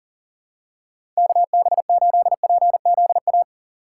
Morse code, heard as a keyed tone, about 40 wpm, 700 Hz. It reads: KB8PZA
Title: USA Call Signs - 6 Characters - 3x Character Spacing - 40wpm